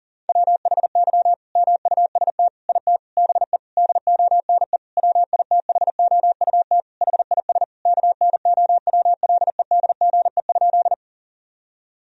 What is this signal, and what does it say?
Morse code, heard as a keyed tone, about 37 wpm, 700 Hz. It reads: WHY MUST IT BE DONE WITHOUT HIS KNOWLEDGE?
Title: Sentences from Top 1000 Words - Review Part 3 - 40wpm